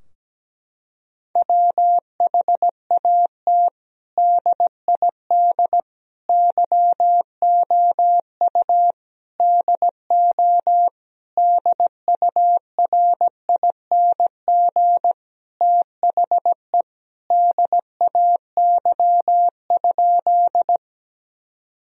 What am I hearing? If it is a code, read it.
Morse code, heard as a keyed tone, about 17 wpm, 700 Hz. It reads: WHAT DID YOU DO DURING THE DAY?
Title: Sentences from Top 1000 Words - Review Part 2 - 17wpm